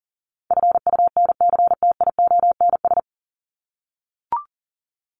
Morse code, heard as a keyed tone, about 38 wpm, 700 Hz. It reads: FUNCTIONS
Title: ES:S Suffix - Top 100 Words with Base Words 40wpm